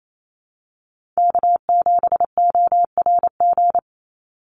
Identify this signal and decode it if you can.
Morse code, heard as a keyed tone, about 28 wpm, 700 Hz. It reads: K7ORG
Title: NA SOTA Chaser Callsigns 28wpm